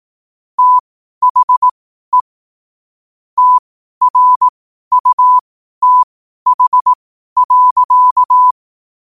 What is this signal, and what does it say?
Morse code, heard as a keyed tone, about 18 wpm, spaced as for 12 wpm, 1000 Hz. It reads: THE TRUTH.
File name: qotdfiles / 12wpm